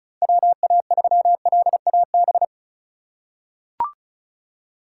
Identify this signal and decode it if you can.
Morse code, heard as a keyed tone, about 36 wpm, 700 Hz. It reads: WA3LAB
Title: K1USN SST QSO Practice - 35wpm